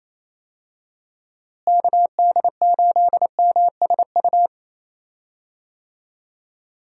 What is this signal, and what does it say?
Morse code, heard as a keyed tone, about 28 wpm, 700 Hz. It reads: KD8MSU
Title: USA Call Signs - 6 Characters - Repeated 3x - 28wpm